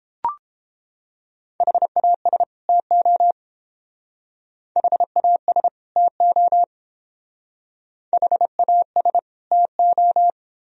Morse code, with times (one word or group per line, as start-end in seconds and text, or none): 1.6-2.4 HAS
2.7-3.3 TO
4.8-5.7 HAS
6.0-6.6 TO
8.1-9.2 HAS
9.5-10.3 TO